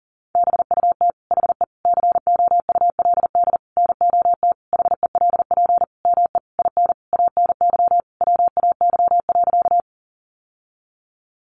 Morse code, heard as a keyed tone, about 40 wpm, 700 Hz. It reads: BUT HE COULD NOT HELP ME IN ANY WAY.
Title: Sentences from Top 600 Words 40wpm